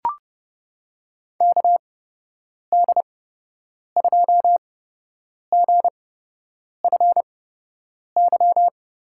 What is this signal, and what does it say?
Morse code, heard as a keyed tone, about 30 wpm, spaced as for 8 wpm, 700 Hz. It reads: KD2GFY
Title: USA Call Signs - 6 Characters - 8x Character Spacing - 30wpm